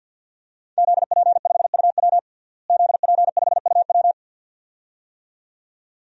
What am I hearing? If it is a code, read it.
Morse code, heard as a keyed tone, about 52 wpm, 700 Hz. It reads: ZP5UW ZP5UW
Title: DXCC-100 Call Signs - 5 Characters - Repeated 2x - 50wpm